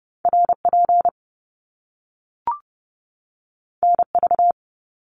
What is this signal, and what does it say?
Morse code, heard as a keyed tone, about 30 wpm, 700 Hz. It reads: RP NV
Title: Top Two Letter Combinations in Words 30wpm